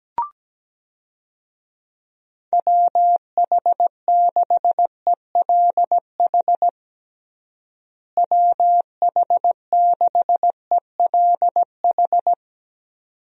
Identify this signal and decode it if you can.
Morse code, heard as a keyed tone, about 17 wpm, 700 Hz. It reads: WH6ELH WH6ELH
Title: USA Call Signs - 6 Characters - Repeated 2x - 17wpm